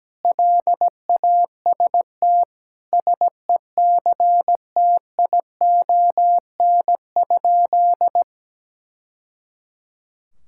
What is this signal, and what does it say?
Morse code, heard as a keyed tone, about 17 wpm, 700 Hz. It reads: LAST SECTION?
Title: Sentences from Top 1000 Words - Review Part 5 - 17wpm